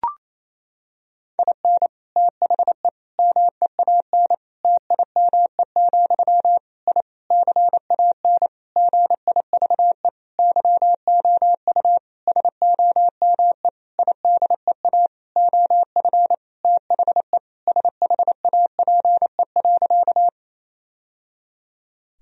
Morse code, heard as a keyed tone, about 28 wpm, 700 Hz. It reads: IN THE MEAN TIME, I CAN GIVE YOU SOME IDEA OF THE SHAPE.
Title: Sentences from Top 1000 Words - Review Part 3 - 28wpm